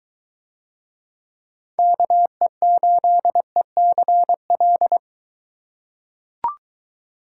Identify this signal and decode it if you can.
Morse code, heard as a keyed tone, about 22 wpm, 700 Hz. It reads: KE8ECL